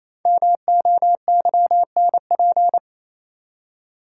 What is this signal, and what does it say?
Morse code, heard as a keyed tone, about 28 wpm, 700 Hz. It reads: MOYNP
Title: Code Groups - 5 Letter:Numbers - 28wpm